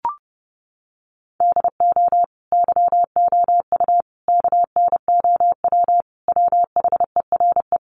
Morse code, heard as a keyed tone, about 30 wpm, 700 Hz. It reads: DO YOU KNOW WHERE
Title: Sentences from Top 1400 Words 30wpm